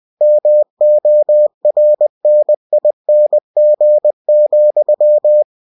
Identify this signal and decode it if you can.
Morse code, heard as a keyed tone, about 20 wpm, 600 Hz. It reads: MORNING,
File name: cw-books / winnie-the-pooh / winnie-20wpm-page-0045